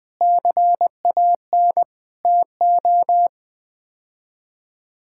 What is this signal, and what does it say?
Morse code, heard as a keyed tone, about 21 wpm, 700 Hz. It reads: CAN TO
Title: Sets of 8 Words - 20wpm